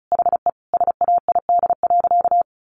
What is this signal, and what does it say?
Morse code, heard as a keyed tone, about 34 wpm, 700 Hz. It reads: HE SAID.
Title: Sentences from Top 1000 Words 35wpm